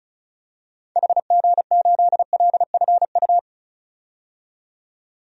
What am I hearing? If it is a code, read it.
Morse code, heard as a keyed tone, about 34 wpm, 700 Hz. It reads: HG8LFU
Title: DXCC-100 Call Signs - 6 Characters - 35wpm